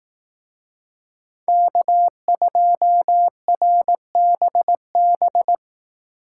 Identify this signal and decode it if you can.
Morse code, heard as a keyed tone, about 19 wpm, 700 Hz. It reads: K2RBB
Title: Speed Racing - USA 1x3 Call Signs 15wpm